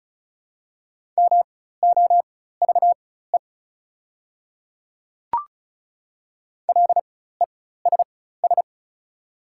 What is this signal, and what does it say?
Morse code, heard as a keyed tone, about 35 wpm, spaced as for 16 wpm, 690 Hz. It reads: MOVE LESS
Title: 4x Character Spacing - Top 500 Words - 35wpm